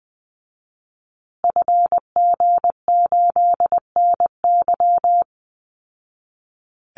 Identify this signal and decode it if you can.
Morse code, heard as a keyed tone, about 20 wpm, 700 Hz. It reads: FG8NY
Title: Call Signs Repeated 2x - 20wpm